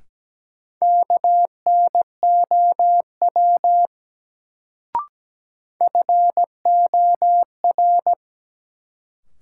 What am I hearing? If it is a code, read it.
Morse code, heard as a keyed tone, about 17 wpm, 700 Hz. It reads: KNOW FOR
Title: Sentences from Top 100 Words - New Word Drill - 17wpm